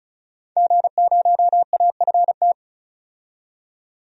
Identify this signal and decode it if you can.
Morse code, heard as a keyed tone, about 36 wpm, 700 Hz. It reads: G0AFT
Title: DXCC-100 Call Signs - 5 Characters - 35wpm